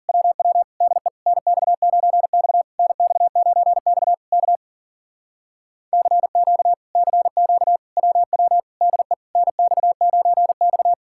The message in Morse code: WW DE NX9X NX9X K CQ CQ WW DE NX9X

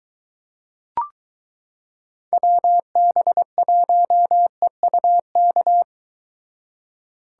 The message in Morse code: WB1EUK

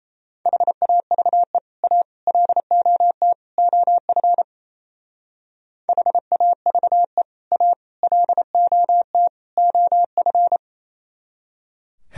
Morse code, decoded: HAVE A LOT OF HAVE A LOT OF

